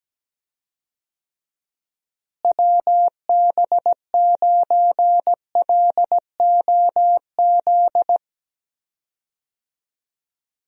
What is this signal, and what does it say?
Morse code, heard as a keyed tone, about 17 wpm, 700 Hz. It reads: WB9LOZ
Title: USA Call Signs - 6 Characters - Repeated 2x - 17wpm